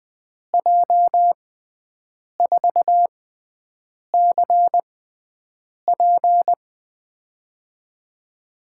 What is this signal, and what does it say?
Morse code, heard as a keyed tone, about 20 wpm, spaced as for 7 wpm, 700 Hz. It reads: J4CP
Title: USA Call Signs - 5 Characters - 6x Character Spacing - 20wpm